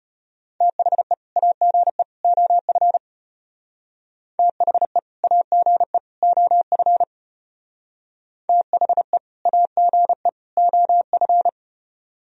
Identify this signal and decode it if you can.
Morse code, heard as a keyed tone, about 30 wpm, 700 Hz. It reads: THE AGE OF THE AGE OF THE AGE OF
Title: Speed-Racing - Sets of 3 Words 25wpm